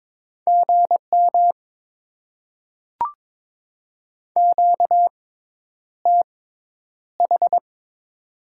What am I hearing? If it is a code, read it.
Morse code, heard as a keyed tone, about 22 wpm, spaced as for 10 wpm, 700 Hz. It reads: GM QTH
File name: Top 20 QSO Elements - 6x Character Spacing - 22wpm